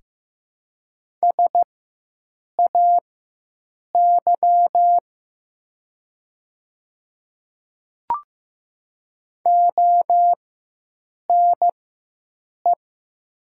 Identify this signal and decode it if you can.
Morse code, heard as a keyed tone, about 15 wpm, spaced as for 7 wpm, 700 Hz. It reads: SAY ONE
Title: 4x Character Spacing - Top 300 Words - 15wpm